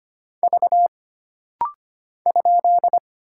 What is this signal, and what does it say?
Morse code, heard as a keyed tone, about 25 wpm, 700 Hz. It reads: V ?
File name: Single Character (Letter:Number:Punctuation:Prosign) - Rapid-Fire - 25wpm